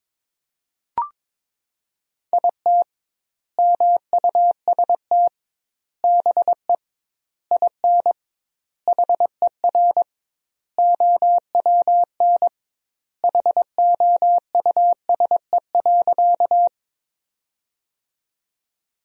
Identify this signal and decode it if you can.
Morse code, heard as a keyed tone, about 21 wpm, 700 Hz. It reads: IT MUST BE IN HER OWN HOUSE.
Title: 2x Word-Spacing - Sentences from Top 300 Words - 22wpm